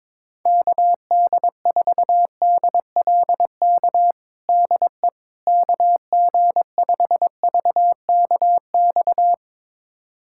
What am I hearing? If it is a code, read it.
Morse code, heard as a keyed tone, about 22 wpm, 700 Hz. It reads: KD4DLK DE KG5VKX
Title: USA Callsign DE Callsign - 6 Characters - 22wpm